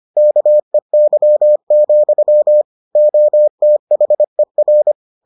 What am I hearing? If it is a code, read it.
Morse code, heard as a keyed tone, about 25 wpm, 600 Hz. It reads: KEY, OTHER